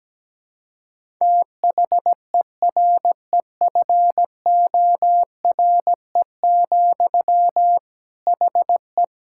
Morse code, heard as a keyed tone, about 17 wpm, 700 Hz. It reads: THEREFORE, HE
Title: Sentences from Top 1000 Words - Review Part 5 - 17wpm